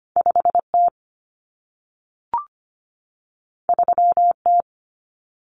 Morse code, heard as a keyed tone, about 25 wpm, 700 Hz. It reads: 5T 3T
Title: Common Cut Numbers - 2-Digit 25wpm